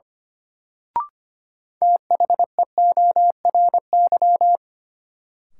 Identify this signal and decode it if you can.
Morse code, heard as a keyed tone, about 25 wpm, 700 Hz. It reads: THEORY
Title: Sentences from Top 600 Words - New Word Drill - 25wpm